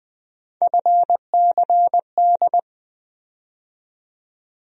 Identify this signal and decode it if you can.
Morse code, heard as a keyed tone, about 20 wpm, 700 Hz. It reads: FCD
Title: USA Call Sign Postfixes - 3 Letters 20wpm